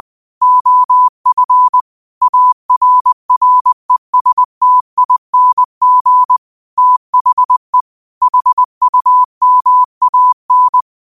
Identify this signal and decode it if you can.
Morse code, heard as a keyed tone, about 20 wpm, 1000 Hz. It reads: OF ARRESTING THE HUMAN
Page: Daily quotes in Morse Code at 20 Words per Minute. Use this to improve your code copying skill.